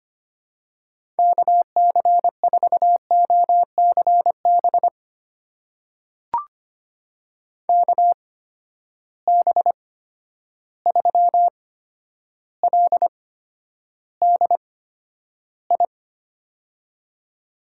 Morse code, heard as a keyed tone, about 25 wpm, spaced as for 11 wpm, 700 Hz. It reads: KC4OCB KB3LDI
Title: USA Call Signs - 6 Characters - 8x Character Spacing - 25wpm